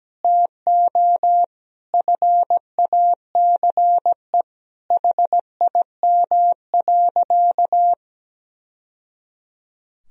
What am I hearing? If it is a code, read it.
Morse code, heard as a keyed tone, about 17 wpm, 700 Hz. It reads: TO FACE HIM.
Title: Sentences from Top 1000 Words - Review Part 2 - 17wpm